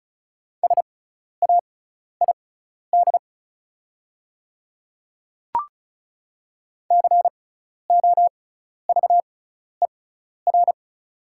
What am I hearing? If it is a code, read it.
Morse code, heard as a keyed tone, about 36 wpm, spaced as for 12 wpm, 690 Hz. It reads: SAID COVER
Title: 6x Character Spacing - Top 300 Words - 35wpm